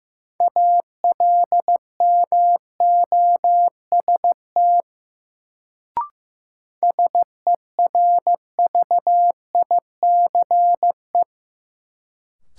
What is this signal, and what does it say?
Morse code, heard as a keyed tone, about 15 wpm, 700 Hz. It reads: ALMOST SERVICE
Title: Sentences from Top 700 Words - New Word Drill - 15wpm